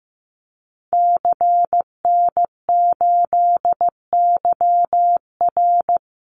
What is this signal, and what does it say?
Morse code, heard as a keyed tone, about 15 wpm, 700 Hz. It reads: CN8YR
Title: Call Signs Repeated 2x - 15wpm